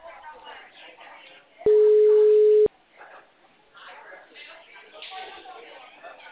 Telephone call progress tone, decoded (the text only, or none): ringback tone